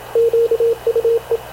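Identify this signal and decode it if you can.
Morse code, heard as a keyed tone, about 26 wpm, 460 Hz, with radio noise. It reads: QUE
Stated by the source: Havana Radio CLA press (PX) broadcast, recorded by Radio Officer Paul Dumesnil aboard mv CNAV QUEST/CZDO.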